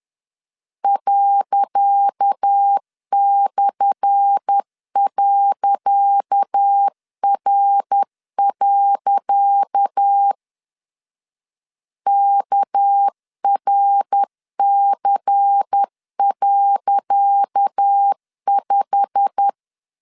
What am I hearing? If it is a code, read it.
Morse code, heard as a keyed tone, about 10 wpm, 800 Hz. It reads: ./.R. KRC.5
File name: LEZIONI / 10 SIGNS / 10C DOT 10wpm 2940